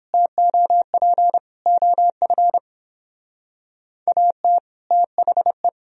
Morse code, heard as a keyed tone, about 31 wpm, 700 Hz. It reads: TOP OF AT THE